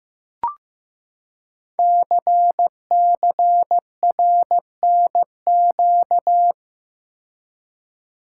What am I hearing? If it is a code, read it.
Morse code, heard as a keyed tone, about 15 wpm, 700 Hz. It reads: CCRNQ
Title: Code Groups - 5 Letter:Numbers - 15wpm